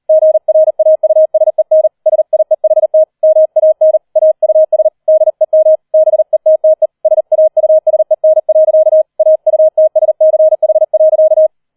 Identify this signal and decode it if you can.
Morse code, heard as a keyed tone, about 38 wpm, 630 Hz. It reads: GRAUSEN SIEHT MAN AUS DEM BETTE SAUSEN. AUTSCH.
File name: 40wpm